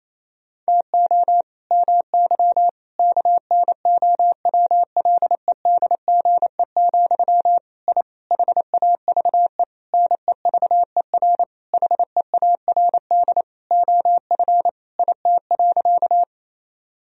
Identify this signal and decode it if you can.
Morse code, heard as a keyed tone, about 28 wpm, 700 Hz. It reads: TO MY KNOWLEDGE, I HAVE NEVER HEARD OF IT.
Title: Sentences from Top 1000 Words - Review Part 3 - 28wpm